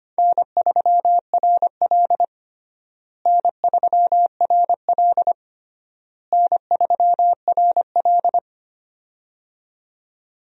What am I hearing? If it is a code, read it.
Morse code, heard as a keyed tone, about 25 wpm, 700 Hz. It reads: N3RL N3RL N3RL
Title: USA Call Signs - 4 Characters - Repeated 3x - 25wpm